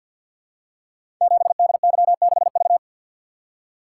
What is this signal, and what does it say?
Morse code, heard as a keyed tone, about 52 wpm, 690 Hz. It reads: 7DY6V
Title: Code Groups - 5 Letter:Numbers - 50wpm